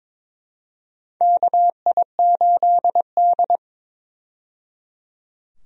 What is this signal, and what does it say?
Morse code, heard as a keyed tone, about 22 wpm, 700 Hz. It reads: KI8D
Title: USA Call Signs - 4 Characters - 22wpm